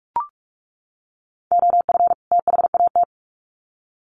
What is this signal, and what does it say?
Morse code, heard as a keyed tone, about 44 wpm, 700 Hz. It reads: OF THAT
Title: Sets of 2 Words 45wpm